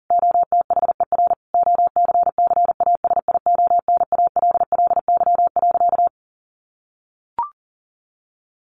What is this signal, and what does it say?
Morse code, heard as a keyed tone, about 37 wpm, 700 Hz. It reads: OTHER OCCASIONALLY.
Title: Sentences from Top 1300 Words 40wpm